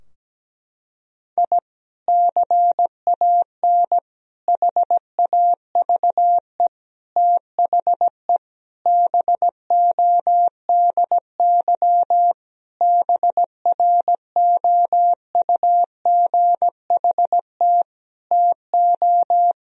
Morse code, heard as a keyed tone, about 17 wpm, 700 Hz. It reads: I CAN HAVE THE BODY BROUGHT TO
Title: Sentences from Top 1000 Words - Review Part 2 - 17wpm